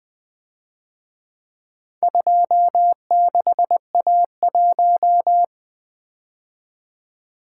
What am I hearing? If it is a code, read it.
Morse code, heard as a keyed tone, about 25 wpm, 700 Hz. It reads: 26A1
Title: Speed-Racing - Four Letter-Number Combinations - 20wpm